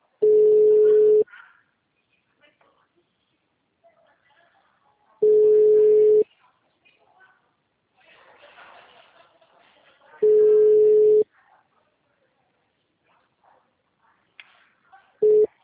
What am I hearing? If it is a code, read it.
Ringback tone (Europe, 425 Hz), with speech.
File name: out- -2013- - - .139